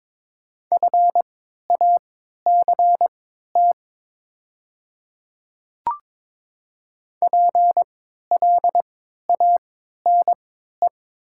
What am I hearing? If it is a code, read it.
Morse code, heard as a keyed tone, about 21 wpm, spaced as for 12 wpm, 700 Hz. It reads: FACT PLANE
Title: 3x Character Spacing - Top 500 Words - 22wpm